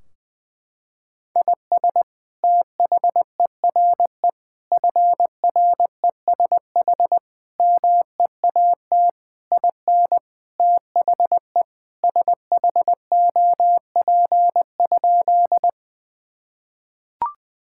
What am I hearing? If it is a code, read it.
Morse code, heard as a keyed tone, about 20 wpm, 700 Hz. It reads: IS THERE FRESH MEAT IN THE SHOP?
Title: Sentences from Top 1600 Words 20wpm